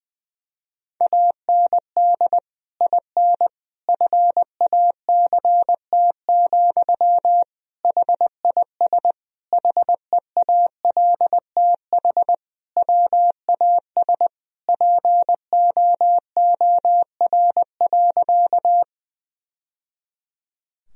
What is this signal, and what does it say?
Morse code, heard as a keyed tone, about 19 wpm, 700 Hz. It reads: AND IN FACT, HIS HEALTH WAS POOR.
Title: Sentences from Top 1000 Words - Review Part 3 - 20wpm